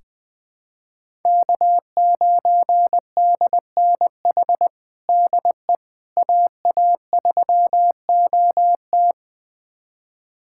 Morse code, heard as a keyed tone, about 21 wpm, 700 Hz. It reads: K9DNH DE AA3OT
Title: USA Callsign DE Callsign - 5 Characters - 20wpm